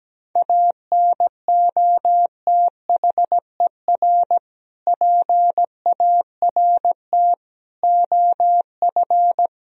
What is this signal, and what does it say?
Morse code, heard as a keyed tone, about 17 wpm, 700 Hz. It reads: ANOTHER PART OF
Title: Sentences from Top 1000 Words - Review Part 2 - 17wpm